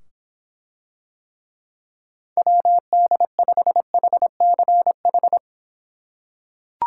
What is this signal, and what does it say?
Morse code, heard as a keyed tone, about 25 wpm, 700 Hz. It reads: WD5HCH